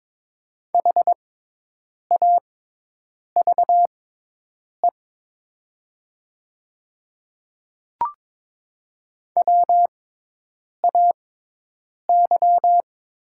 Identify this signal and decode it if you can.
Morse code, heard as a keyed tone, about 22 wpm, spaced as for 8 wpm, 700 Hz. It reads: HAVE WAY
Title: 6x Character Spacing - Top 100 Words - 22wpm